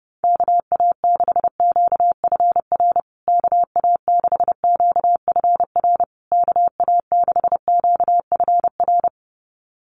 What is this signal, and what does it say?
Morse code, heard as a keyed tone, about 30 wpm, 700 Hz. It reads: KA6QFR KA6QFR KA6QFR
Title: USA Call Signs Repeated 3x 30wpm